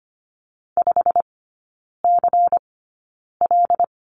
Morse code, heard as a keyed tone, about 25 wpm, spaced as for 9 wpm, 700 Hz. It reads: 5CL
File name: course / 25:10wpm / 093 Lesson 16e - Letter c - Three Characters 25wpm